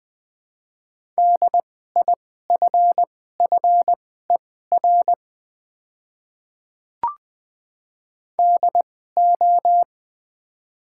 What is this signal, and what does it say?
Morse code, heard as a keyed tone, about 20 wpm, spaced as for 14 wpm, 700 Hz. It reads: DIFFER DO
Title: 2x Character Spacing - Top 300 Words - 20wpm